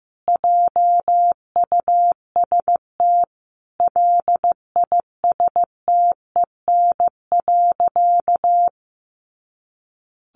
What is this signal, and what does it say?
Morse code, heard as a keyed tone, about 15 wpm, 700 Hz. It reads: JUST LISTEN.
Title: Sentences from Top 500 Words 15wpm